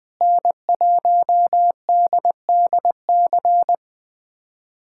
Morse code, heard as a keyed tone, about 20 wpm, 700 Hz. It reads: N1DDC
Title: USA Callsign DE Callsign - 5 Characters - 20wpm